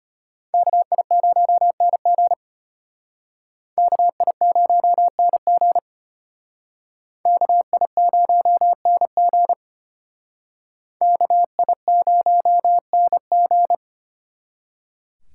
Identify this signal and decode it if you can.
Morse code, heard as a keyed tone, about 31 wpm, 700 Hz. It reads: KI0NG KI0NG KI0NG KI0NG
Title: Speed Racing - USA 2x2 Call Signs 25wpm